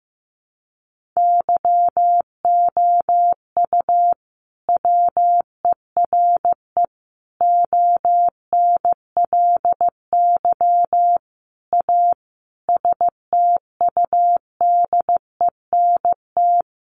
Morse code, heard as a keyed tone, about 14 wpm, 700 Hz. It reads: YOU WERE ONLY A STUDENT
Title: Sentences from Top 1000 Words 15wpm